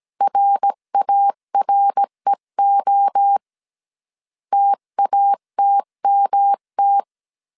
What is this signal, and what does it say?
Morse code, heard as a keyed tone, about 17 wpm, 800 Hz. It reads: RAREO TATMT
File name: CW / DET / 1 3 1500C S A 16wpm 2141